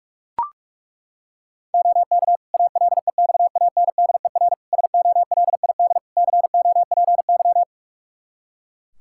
Morse code, heard as a keyed tone, about 42 wpm, 700 Hz. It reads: OK ALEXANDER SOLID COPY
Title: K1USN SST QSO Practice 45wpm